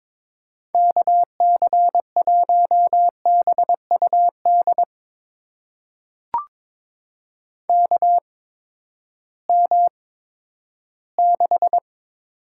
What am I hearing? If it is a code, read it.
Morse code, heard as a keyed tone, about 22 wpm, spaced as for 12 wpm, 700 Hz. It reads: KC1BUD KM6
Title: USA Call Signs - 6 Characters - 8x Character Spacing - 22wpm